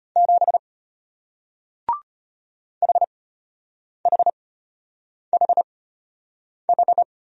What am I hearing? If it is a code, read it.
Morse code, heard as a keyed tone, about 34 wpm, spaced as for 8 wpm, 700 Hz. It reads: 7 HHHH